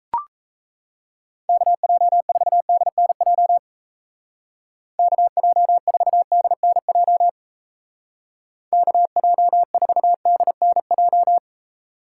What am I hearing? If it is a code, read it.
Morse code, heard as a keyed tone, about 35 wpm, 700 Hz. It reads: KJ4DNJ KJ4DNJ KJ4DNJ
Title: Speed Racing - USA 2x3 Call Signs 28wpm